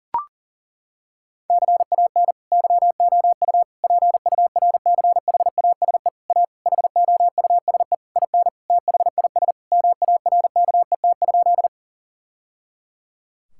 Morse code, heard as a keyed tone, about 40 wpm, 700 Hz. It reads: CAN YOU PURCHASE A HOUSE IN THIS MARKET?
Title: Sentences from Top 1000 Words - Review Part 5 - 40wpm